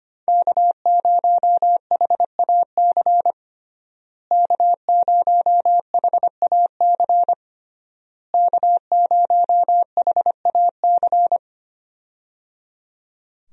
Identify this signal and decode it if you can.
Morse code, heard as a keyed tone, about 25 wpm, 700 Hz. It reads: K0HAC K0HAC K0HAC
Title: USA Call Signs - 5 Characters - Repeated 3x - 25wpm